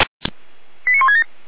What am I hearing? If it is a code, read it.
Five-tone selective call (ZVEI), heard as quick runs of five tones, tones of about 80 ms. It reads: 89167
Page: /455/SDR/ZVEI/